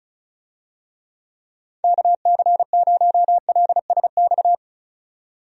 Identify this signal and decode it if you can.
Morse code, heard as a keyed tone, about 36 wpm, 700 Hz. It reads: KC0LSX
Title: USA Call Signs - 6 Characters - 35wpm